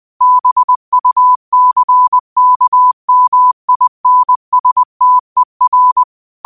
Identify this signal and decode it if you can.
Morse code, heard as a keyed tone, about 20 wpm, 1000 Hz. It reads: BUCKMINSTER
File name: qotdfiles / 20wpm